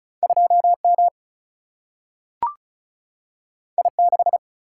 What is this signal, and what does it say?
Morse code, heard as a keyed tone, about 36 wpm, 700 Hz. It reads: 2 M I 6
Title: Single-Letter Single-Number 35wpm